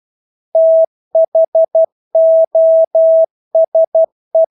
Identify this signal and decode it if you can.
Morse code, heard as a keyed tone, about 12 wpm, 650 Hz. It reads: THOSE
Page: Engelske ord 12 wpm